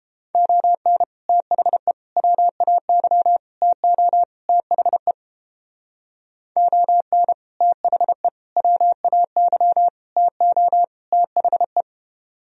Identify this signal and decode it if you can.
Morse code, heard as a keyed tone, about 29 wpm, 700 Hz. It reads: ON THE WAY TO THE ON THE WAY TO THE